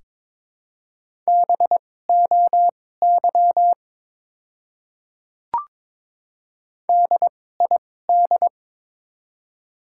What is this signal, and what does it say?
Morse code, heard as a keyed tone, about 22 wpm, spaced as for 16 wpm, 700 Hz. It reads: BOY DID
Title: 2x Character Spacing - Top 200 Words - 22wpm